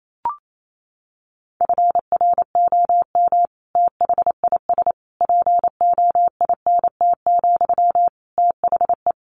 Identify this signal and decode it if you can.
Morse code, heard as a keyed tone, about 28 wpm, 700 Hz. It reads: FROM THIS POINT, THE
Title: Sentences from Top 1200 Words 28wpm